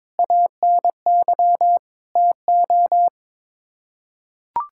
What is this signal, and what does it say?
Morse code, heard as a keyed tone, about 21 wpm, 700 Hz. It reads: ANY TO
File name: Sets of 9 Words - 22wpm